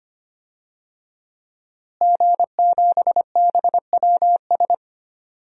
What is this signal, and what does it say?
Morse code, heard as a keyed tone, about 25 wpm, 700 Hz. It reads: G7BWS
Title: DXCC-100 Call Signs - 5 Characters - Repeated 3x - 25wpm